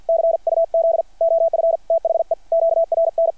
Morse code, heard as a keyed tone, about 53 wpm, 660 Hz. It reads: C UZ OF THE QRN
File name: archived / CRSnet